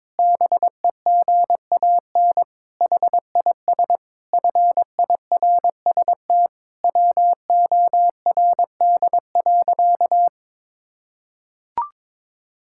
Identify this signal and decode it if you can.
Morse code, heard as a keyed tone, about 22 wpm, 700 Hz. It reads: BEGAN HIS FIRST WORD.
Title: Pangram Challenge v2 - 22wpm